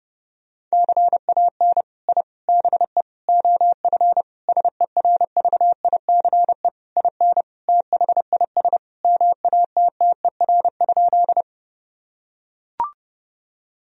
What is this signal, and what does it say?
Morse code, heard as a keyed tone, about 28 wpm, 700 Hz. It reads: CAN I BE OF SERVICE IN THIS MATTER?
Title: Sentences from Top 1000 Words - Review Part 4 - 30wpm